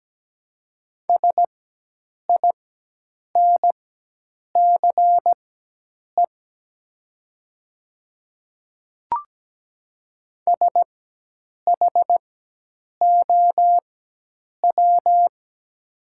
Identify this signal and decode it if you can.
Morse code, heard as a keyed tone, about 17 wpm, spaced as for 8 wpm, 700 Hz. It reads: SINCE SHOW
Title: 4x Character Spacing - Top 300 Words - 17wpm